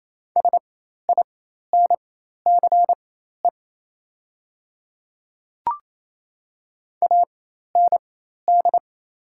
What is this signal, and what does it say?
Morse code, heard as a keyed tone, about 28 wpm, spaced as for 13 wpm, 700 Hz. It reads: SINCE AND
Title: 4x Character Spacing - Top 300 Words - 28wpm